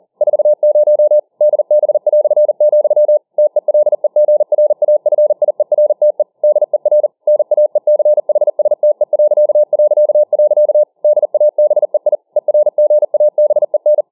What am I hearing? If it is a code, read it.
Morse code, heard as a keyed tone, about 40 wpm, 600 Hz. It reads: 40 DB", TELEGRAFIERTE DER NAECHSTE... DABEI ERGABEN